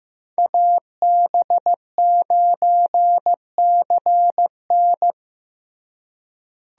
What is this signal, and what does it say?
Morse code, heard as a keyed tone, about 15 wpm, 700 Hz. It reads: AB9CN
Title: USA Call Signs - 5 Characters - Repeated 3x - 15wpm